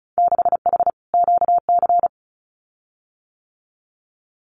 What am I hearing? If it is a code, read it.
Morse code, heard as a keyed tone, about 34 wpm, 700 Hz. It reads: 6H QC
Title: Winter Field Day Exchanges Encore 35wpm